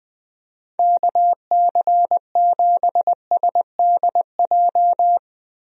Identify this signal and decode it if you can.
Morse code, heard as a keyed tone, about 21 wpm, 700 Hz. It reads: KC7SDJ
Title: USA Call Signs - 6 Characters - Repeated 3x - 20wpm